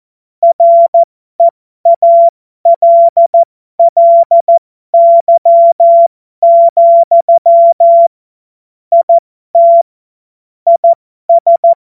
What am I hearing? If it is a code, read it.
Morse code, heard as a keyed tone, about 14 wpm, spaced as for 12 wpm, 680 Hz. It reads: REALLY, IT IS